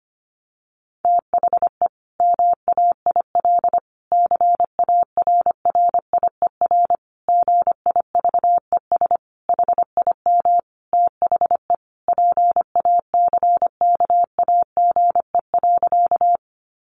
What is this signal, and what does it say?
Morse code, heard as a keyed tone, about 25 wpm, 700 Hz. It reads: THE MAIL CARRIER GIVES HIM THE PACKAGE.
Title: Additional Easy Phrases 25wpm